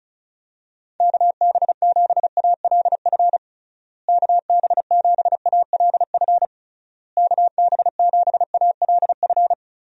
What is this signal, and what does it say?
Morse code, heard as a keyed tone, about 34 wpm, 700 Hz. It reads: KB7ALF KB7ALF KB7ALF
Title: USA Call Signs - 6 Characters - Repeated 3x - 35wpm